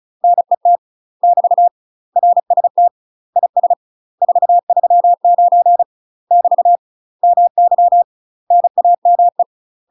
Morse code, heard as a keyed tone, about 36 wpm, 700 Hz. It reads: NET <BT> RST IS 439 <BT> MY NAME
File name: UK MASTER QSOs TRAINING / 35WPM / UK MASTER QSOs 0009